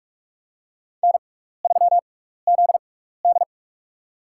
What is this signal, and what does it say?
Morse code, heard as a keyed tone, about 48 wpm, spaced as for 16 wpm, 700 Hz. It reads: N3ZD
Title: USA Call Signs - 4 Characters - 6x Character Spacing - 45wpm